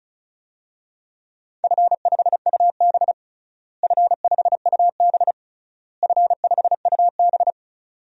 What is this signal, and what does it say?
Morse code, heard as a keyed tone, about 34 wpm, 700 Hz. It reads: F5UB F5UB F5UB
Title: DXCC-100 Call Signs - 4 Characters - Repeated 3x - 35wpm